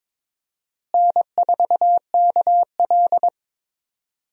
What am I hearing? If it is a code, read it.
Morse code, heard as a keyed tone, about 22 wpm, 700 Hz. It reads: N4KL
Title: USA Call Signs - 4 Characters - Repeated 2x - 22wpm